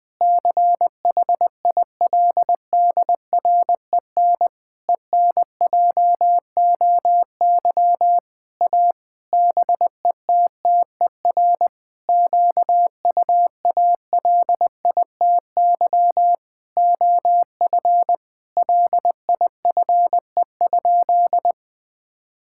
Morse code, heard as a keyed tone, about 20 wpm, 700 Hz. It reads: CHILDREN ENJOY A BETTER QUALITY OF LIFE?
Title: Sentences from Top 1000 Words - Review Part 5 - 20wpm